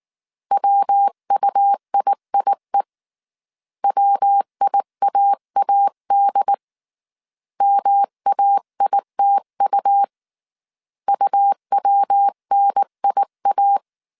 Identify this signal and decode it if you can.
Morse code, heard as a keyed tone, about 19 wpm, 800 Hz. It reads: WUIIE WIAAD MAITU UWNIA
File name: CW / DET / 1 3 1500C S B 18wpm 1916